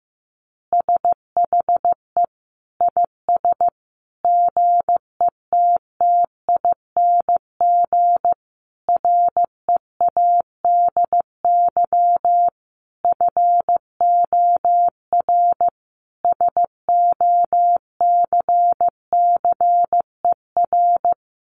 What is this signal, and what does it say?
Morse code, heard as a keyed tone, about 14 wpm, 700 Hz. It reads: SHE IS GETTING READY FOR SOCCER
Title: Easy Phrases 15wpm